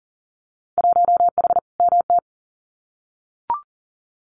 Morse code, heard as a keyed tone, about 40 wpm, 700 Hz. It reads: 1H MT
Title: Winter Field Day Exchanges Encore 40wpm